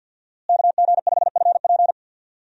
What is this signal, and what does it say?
Morse code, heard as a keyed tone, about 52 wpm, 690 Hz. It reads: KG5FP
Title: USA Call Signs - 5 Characters - Repeated 3x - 50wpm